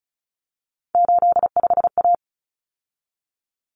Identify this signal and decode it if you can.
Morse code, heard as a keyed tone, about 34 wpm, 700 Hz. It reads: 85A
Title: Common Cut Numbers - 3-Digit 35wpm